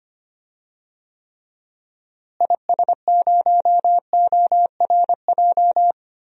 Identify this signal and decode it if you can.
Morse code, heard as a keyed tone, about 25 wpm, 700 Hz. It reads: IS0ORJ
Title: DXCC-100 Call Signs - 6 Characters - Repeated 2x - 25wpm